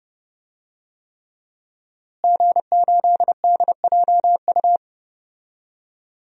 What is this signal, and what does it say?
Morse code, heard as a keyed tone, about 30 wpm, 700 Hz. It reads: G8DJU
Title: DXCC-100 Call Signs - 5 Characters - Repeated 3x - 30wpm